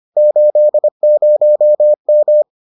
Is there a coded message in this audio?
Yes: Morse code from 0.2 to 2.4 s, heard as a keyed tone, about 25 wpm, 600 Hz. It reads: 80M